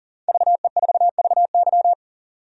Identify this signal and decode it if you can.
Morse code, heard as a keyed tone, about 39 wpm, 700 Hz. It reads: VE4VY
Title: DXCC-100 Call Signs - 5 Characters - 40wpm